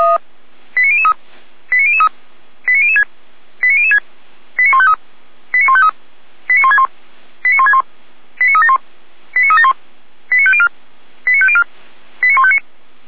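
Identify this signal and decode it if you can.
Five-tone selective call (ZVEI), heard as quick runs of five tones, tones of about 70 ms. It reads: 89003 89003 89006 89006 89153 89153 89161 89161 89371 89371 89504 89504 89159
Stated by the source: ZVEI five-tone selective call received by SDR, fire brigade Orlamünde (feuerwehr-orlamuende).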